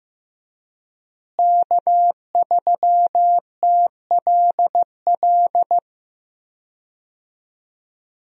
Morse code, heard as a keyed tone, about 16 wpm, 700 Hz. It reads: K3TLL
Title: USA Call Signs - 5 Characters - Repeated 3x - 15wpm